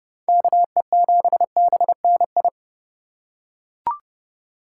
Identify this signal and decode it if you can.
Morse code, heard as a keyed tone, about 28 wpm, 700 Hz. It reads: KE7BNI